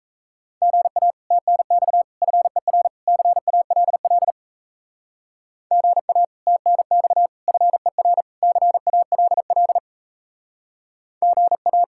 Morse code, heard as a keyed tone, about 34 wpm, 700 Hz. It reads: GA TNX FER CALL GA TNX FER CALL GA
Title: Speed-Racing Common QSO Phrases - 28wpm